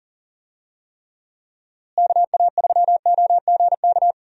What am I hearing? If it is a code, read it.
Morse code, heard as a keyed tone, about 37 wpm, 700 Hz. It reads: KA3OGK
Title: USA Call Signs - 6 Characters - Repeated 3x - 40wpm